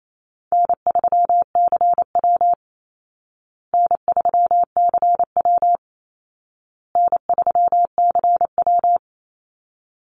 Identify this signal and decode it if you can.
Morse code, heard as a keyed tone, about 28 wpm, 700 Hz. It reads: N3CW N3CW N3CW
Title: Easy Call Signs Repeated 3x - 28wpm